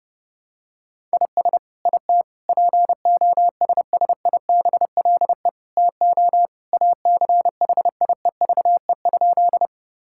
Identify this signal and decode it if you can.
Morse code, heard as a keyed tone, about 30 wpm, 700 Hz. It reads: IS IT POSSIBLE TO ACHIEVE?
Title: Sentences from Top 1000 Words - Review Part 6 - 30wpm